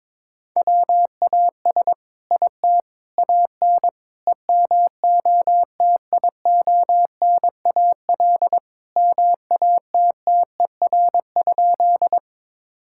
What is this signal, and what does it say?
Morse code, heard as a keyed tone, about 22 wpm, 700 Hz. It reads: WAS IT AN EMOTIONAL MATTER?
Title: Sentences from Top 1000 Words - Review Part 5 - 22wpm